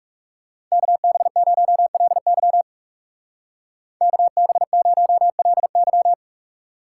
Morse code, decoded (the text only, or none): KB0LY KB0LY